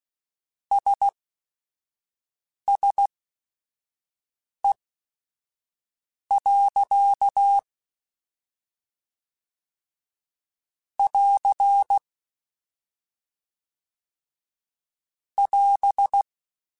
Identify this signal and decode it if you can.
Morse code, heard as a keyed tone, about 16 wpm, spaced as for 5 wpm, 800 Hz. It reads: SSE. <AR> <AS>
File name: Weather Sets / Weather Sets-mp3s / Weather 4 14 @5